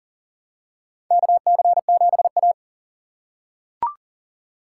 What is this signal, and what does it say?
Morse code, heard as a keyed tone, about 40 wpm, 700 Hz. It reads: KC7A
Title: USA Call Signs - 4 Characters - 3x Character Spacing - 40wpm